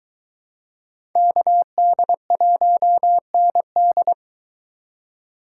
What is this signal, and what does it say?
Morse code, heard as a keyed tone, about 24 wpm, 700 Hz. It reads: KD1ND